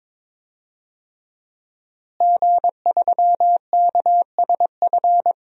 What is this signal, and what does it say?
Morse code, heard as a keyed tone, about 22 wpm, 700 Hz. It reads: G3KSF
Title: DXCC-100 Call Signs - 5 Characters - Repeated 3x - 22wpm